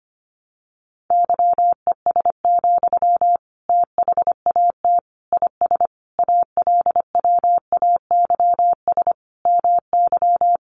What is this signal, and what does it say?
Morse code, heard as a keyed tone, about 25 wpm, 700 Hz. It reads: YES, THAT IS ALWAYS MY
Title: Sentences from Top 1500 Words 25wpm